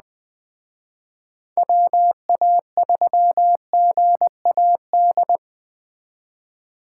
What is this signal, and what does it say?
Morse code, heard as a keyed tone, about 20 wpm, 700 Hz. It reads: WA3GAD